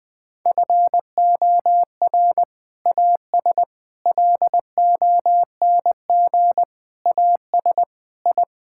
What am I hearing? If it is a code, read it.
Morse code, heard as a keyed tone, about 21 wpm, 700 Hz. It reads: FOR AS LONG AS I
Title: Sentences with 7-Word Phrases - 20wpm